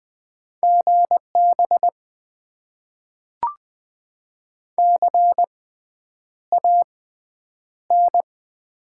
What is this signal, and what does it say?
Morse code, heard as a keyed tone, about 20 wpm, spaced as for 9 wpm, 700 Hz. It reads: GB CAN
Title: Top 90 QSO Elements - 6x Character Spacing - 20wpm